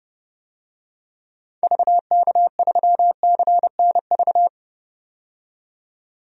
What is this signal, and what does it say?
Morse code, heard as a keyed tone, about 32 wpm, 700 Hz. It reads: VK3CNV
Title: DXCC-100 Call Signs - 6 Characters - 30wpm